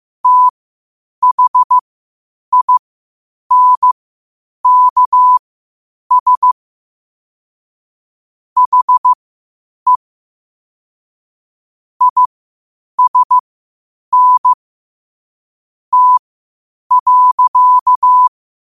Morse code, heard as a keyed tone, about 15 wpm, spaced as for 8 wpm, 1000 Hz. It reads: THINKS HE ISN T.
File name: qotdfiles / 8wpm